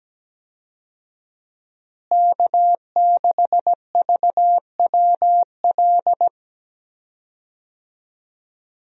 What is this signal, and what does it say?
Morse code, heard as a keyed tone, about 17 wpm, 700 Hz. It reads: K6VWL
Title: USA Call Signs - 5 Characters - Repeated 2x - 17wpm